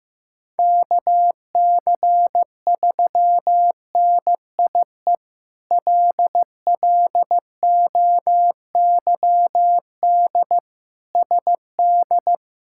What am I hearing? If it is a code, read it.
Morse code, heard as a keyed tone, about 15 wpm, 700 Hz. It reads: KC3NIE LLOYD SD
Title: K1USN SST QSO Practice - 15wpm